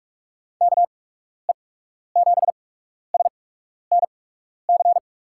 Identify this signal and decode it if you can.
Morse code, heard as a keyed tone, about 44 wpm, spaced as for 12 wpm, 700 Hz. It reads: KE7SNC
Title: USA Call Signs - 6 Characters - 8x Character Spacing - 45wpm